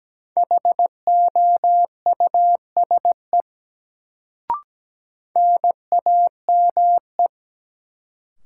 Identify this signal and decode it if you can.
Morse code, heard as a keyed tone, about 17 wpm, 700 Hz. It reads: HOUSE NAME
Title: Sentences from Top 200 Words - New Word Drill - 17wpm